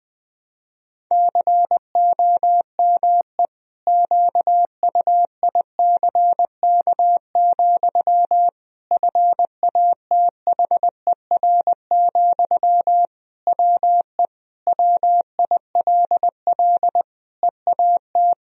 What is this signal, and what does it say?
Morse code, heard as a keyed tone, about 20 wpm, 700 Hz. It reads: COME QUICK, FATHER, WE WILL EAT
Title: Sentences from Top 1000 Words - Review Part 3 - 20wpm